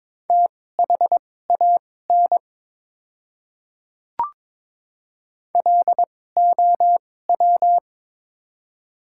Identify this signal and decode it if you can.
Morse code, heard as a keyed tone, about 22 wpm, spaced as for 16 wpm, 700 Hz. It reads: THAN LOW
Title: 2x Character Spacing - Top 200 Words - 22wpm